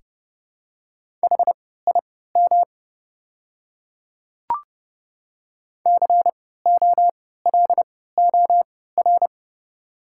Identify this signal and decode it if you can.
Morse code, heard as a keyed tone, about 30 wpm, spaced as for 17 wpm, 690 Hz. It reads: HIM COLOR
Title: 3x Character Spacing - Top 300 Words - 30wpm